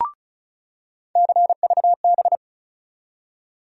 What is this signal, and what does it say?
Morse code, heard as a keyed tone, about 36 wpm, 700 Hz. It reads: CVB
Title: USA Call Sign Postfixes - 3 Letters 35wpm